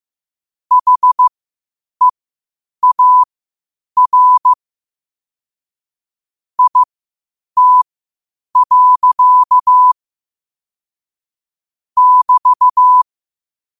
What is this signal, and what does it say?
Morse code, heard as a keyed tone, about 15 wpm, spaced as for 8 wpm, 1000 Hz. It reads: HEAR IT. <BT>
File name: qotdfiles / 8wpm